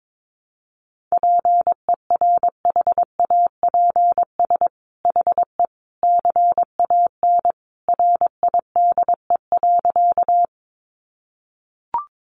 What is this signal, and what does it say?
Morse code, heard as a keyed tone, about 22 wpm, 700 Hz. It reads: PERHAPS HE CAN RIDE.
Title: Sentences from Top 1200 Words 22wpm